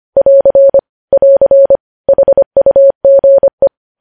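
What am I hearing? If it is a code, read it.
Morse code, heard as a keyed tone, about 25 wpm, 550 Hz. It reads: <AR> <AR> HUGE
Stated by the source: Morse practice of news headlines at 25wpm.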